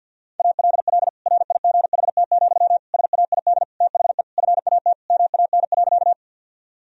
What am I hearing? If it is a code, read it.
Morse code, heard as a keyed tone, about 46 wpm, 700 Hz. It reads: ALL RIGHT, SAID THE FAT MAN.
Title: Sentences from Top 1000 Words - Review Part 4 - 50wpm